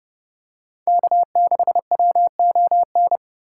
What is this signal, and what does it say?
Morse code, heard as a keyed tone, about 30 wpm, 700 Hz. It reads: K6WON